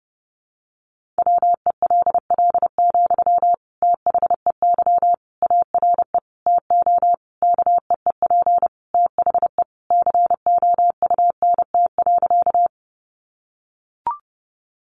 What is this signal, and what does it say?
Morse code, heard as a keyed tone, about 30 wpm, 700 Hz. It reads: WELL, THEY ARE TO KEEP THE COUNT.
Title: Sentences from Top 1500 Words 30wpm